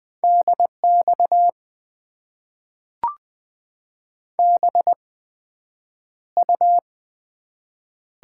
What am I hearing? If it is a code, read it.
Morse code, heard as a keyed tone, about 20 wpm, spaced as for 9 wpm, 700 Hz. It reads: DX BU
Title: Top 90 QSO Elements - 8x Character Spacing - 20wpm